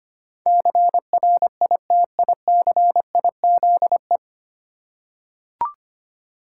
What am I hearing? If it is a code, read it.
Morse code, heard as a keyed tone, about 25 wpm, 700 Hz. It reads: CRITICIZE
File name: practice / 25wpm / 9 Letter Words - First 3 Letters Given - 25wpm